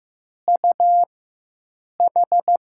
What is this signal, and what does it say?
Morse code, heard as a keyed tone, about 15 wpm, spaced as for 7 wpm, 700 Hz. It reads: UH
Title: USA Call Signs - 5 Characters - 4x Character Spacing - 15wpm